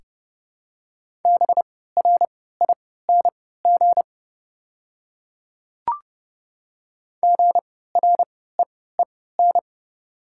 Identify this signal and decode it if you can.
Morse code, heard as a keyed tone, about 30 wpm, spaced as for 17 wpm, 700 Hz. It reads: BRING GREEN
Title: 3x Character Spacing - Top 500 Words - 30wpm